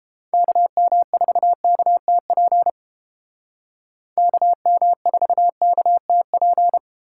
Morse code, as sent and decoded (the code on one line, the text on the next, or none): -.- -- ....- -.- - .--. / -.- -- ....- -.- - .--.
KM4KTP KM4KTP